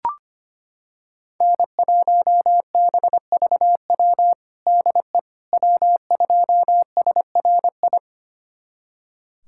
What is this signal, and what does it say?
Morse code, heard as a keyed tone, about 25 wpm, 700 Hz. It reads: N1BVW DE W2SRI
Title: USA Callsign DE Callsign - 5 Characters - 25wpm